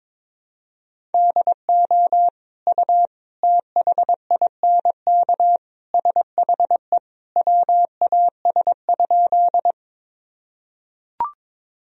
Morse code, heard as a keyed tone, about 22 wpm, 700 Hz. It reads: DO U THINK SHE WAS?
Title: Sentences from Top 200 Words and Abbreviations 22wpm